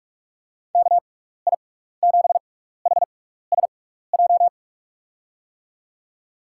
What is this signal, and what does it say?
Morse code, heard as a keyed tone, about 43 wpm, 700 Hz. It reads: K I 7 H S J
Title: USA Call Signs - 6 Characters - 6x Character Spacing - 45wpm